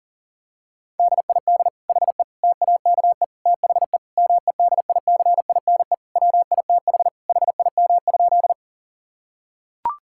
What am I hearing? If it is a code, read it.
Morse code, heard as a keyed tone, about 40 wpm, 700 Hz. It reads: DID HE TAKE THE MEDICINE WITH HIM?
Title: Sentences from Top 1000 Words - Review Part 6 - 40wpm